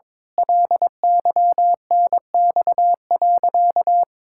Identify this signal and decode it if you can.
Morse code, heard as a keyed tone, about 22 wpm, 700 Hz. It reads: LYNX.
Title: Pangram Challenge v2 - 22wpm